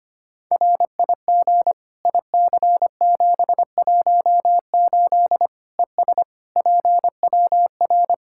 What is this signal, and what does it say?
Morse code, heard as a keyed tone, about 25 wpm, 700 Hz. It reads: RIG IC718 ES PWR
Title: Practice QSO - Exchange 2 - Info - 25wpm